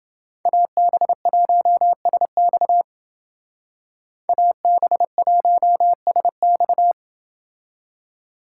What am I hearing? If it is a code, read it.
Morse code, heard as a keyed tone, about 29 wpm, 700 Hz. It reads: AB1SX AB1SX